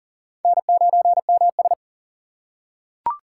Morse code, heard as a keyed tone, about 40 wpm, 700 Hz. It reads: N9MS
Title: K1USN SST QSO Practice - 40wpm